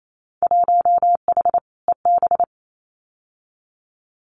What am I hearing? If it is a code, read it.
Morse code, heard as a keyed tone, about 28 wpm, 700 Hz. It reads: 1H EB
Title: Winter Field Day Exchanges 28wpm